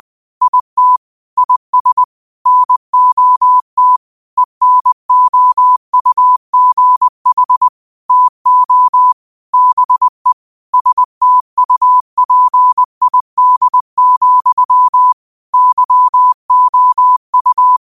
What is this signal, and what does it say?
Morse code, heard as a keyed tone, about 20 wpm, 1000 Hz. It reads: IT IS NOT ENOUGH TO BE STUPID, YOU